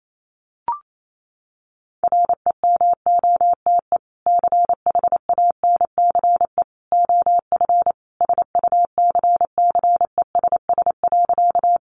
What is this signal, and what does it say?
Morse code, heard as a keyed tone, about 28 wpm, 700 Hz. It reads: REMOTE CHANCE OF SUCCESS.
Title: Sentences from Top 1300 Words 28wpm